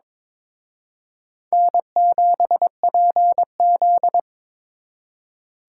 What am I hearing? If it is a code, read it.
Morse code, heard as a keyed tone, about 26 wpm, 700 Hz. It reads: N7PZ
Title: Speed Racing - USA 1x2 Call Signs 22wpm